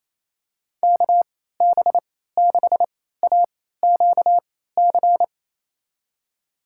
Morse code, heard as a keyed tone, about 28 wpm, spaced as for 16 wpm, 700 Hz. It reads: KB6AQC
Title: USA Call Signs - 6 Characters - 3x Character Spacing - 28wpm